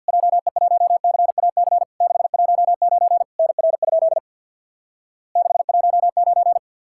Morse code, heard as a keyed tone, about 50 wpm, 700 Hz. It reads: JE1CAC 619 NR? 619
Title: Worked All Europe DX Contest - QTC Exchange with Repeats - 50wpm